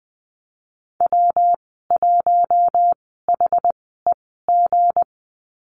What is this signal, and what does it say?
Morse code, heard as a keyed tone, about 20 wpm, spaced as for 15 wpm, 700 Hz. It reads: W1HEG
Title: 173 Lesson 28g - Number 9 - Call Signs 20wpm